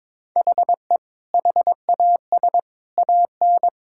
Morse code, heard as a keyed tone, about 22 wpm, 700 Hz. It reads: HE HAS AN
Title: Sentences from Top 1000 Words - Review Part 6 - 22wpm